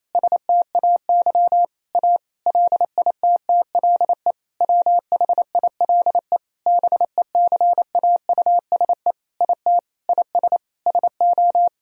Morse code, heard as a keyed tone, about 28 wpm, 700 Hz. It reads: STAY A LITTLE WHILE BECAUSE IT IS SO
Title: Sentences from Top 1000 Words - Review Part 3 - 28wpm